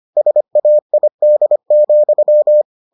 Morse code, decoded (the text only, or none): SAID,